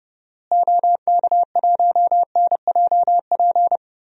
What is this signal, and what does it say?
Morse code, heard as a keyed tone, about 30 wpm, 700 Hz. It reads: OK1NJP
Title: DXCC-100 Call Signs - 6 Characters - Repeated 3x - 30wpm